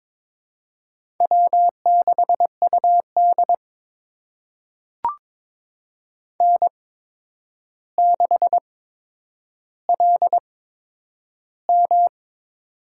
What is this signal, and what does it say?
Morse code, heard as a keyed tone, about 22 wpm, spaced as for 9 wpm, 700 Hz. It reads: W6UD N6LM
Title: USA Call Signs - 4 Characters - 8x Character Spacing - 22wpm